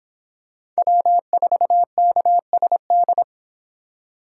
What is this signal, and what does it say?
Morse code, heard as a keyed tone, about 26 wpm, 700 Hz. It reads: W4KSD